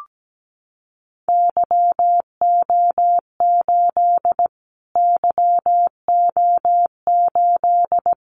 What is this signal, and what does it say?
Morse code, heard as a keyed tone, about 17 wpm, 700 Hz. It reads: YO8 YO8
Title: Callsign Prefixes for Top 25 DXCC - Repeated 3x - 17wpm